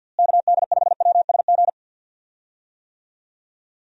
Morse code, heard as a keyed tone, about 50 wpm, 700 Hz. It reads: KD5WSG
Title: USA Call Signs - 6 Characters - Repeated 3x - 50wpm